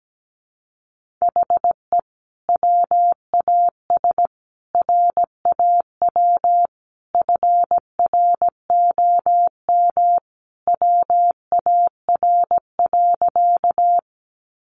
Morse code, heard as a keyed tone, about 17 wpm, 700 Hz. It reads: HE WAS RAW FROM WAR.
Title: Sentences from Top 1500 Words 17wpm